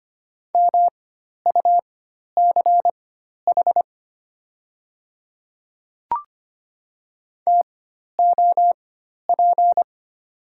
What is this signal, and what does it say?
Morse code, heard as a keyed tone, about 25 wpm, spaced as for 12 wpm, 700 Hz. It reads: MUCH TOP
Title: 4x Character Spacing - Top 500 Words - 25wpm